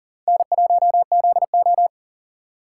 Morse code, heard as a keyed tone, about 50 wpm, 700 Hz. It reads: N1ZO